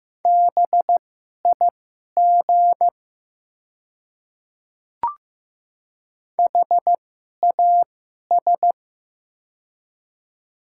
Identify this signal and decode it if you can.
Morse code, heard as a keyed tone, about 15 wpm, spaced as for 11 wpm, 700 Hz. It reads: BIG HAS
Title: 2x Character Spacing - Top 300 Words - 15wpm